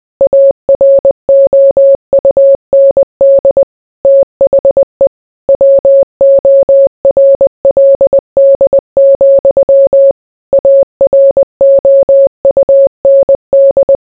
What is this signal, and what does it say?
Morse code, heard as a keyed tone, about 20 wpm, 550 Hz. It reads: AROUND THE WORLD, AROUND